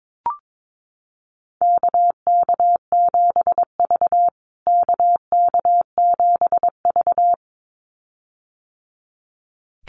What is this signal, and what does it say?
Morse code, heard as a keyed tone, about 22 wpm, 700 Hz. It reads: KK7V KK7V
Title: USA Call Signs Repeated 2x 22wpm